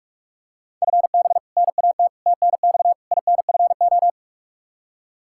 Morse code, heard as a keyed tone, about 46 wpm, 700 Hz. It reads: FB NAT TNX INFO
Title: Practice QSO - Exchange 3 - More Info - 45wpm